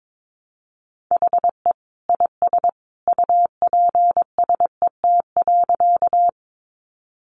Morse code, heard as a keyed tone, about 22 wpm, 700 Hz. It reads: HE IS UPSET.
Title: Additional Easy Phrases 22wpm